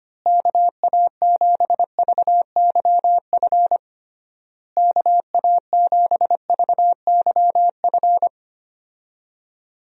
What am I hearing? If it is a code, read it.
Morse code, heard as a keyed tone, about 25 wpm, 700 Hz. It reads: KA7VYF KA7VYF
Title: USA Call Signs - 6 Characters - Repeated 3x - 25wpm